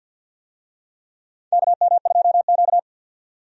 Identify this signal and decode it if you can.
Morse code, heard as a keyed tone, about 53 wpm, 700 Hz. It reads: KM2Q
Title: USA Call Signs - 4 Characters - 50wpm